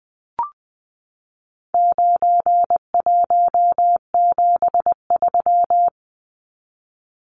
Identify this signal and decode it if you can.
Morse code, heard as a keyed tone, about 20 wpm, 700 Hz. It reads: 9173
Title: Numbers - 4-Digit 20wpm